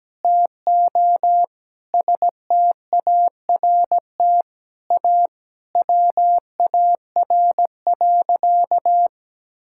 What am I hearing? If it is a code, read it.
Morse code, heard as a keyed tone, about 17 wpm, 700 Hz. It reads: TO START A WAR.
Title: Sentences from Top 1000 Words - Review Part 6 - 17wpm